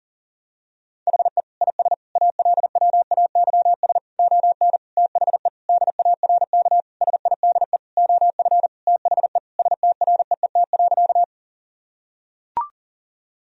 Morse code, heard as a keyed tone, about 40 wpm, 700 Hz. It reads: HE IS ALWAYS ON THE DARK SIDE OF THE STREET.
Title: Sentences from Top 1000 Words - Review Part 3 - 40wpm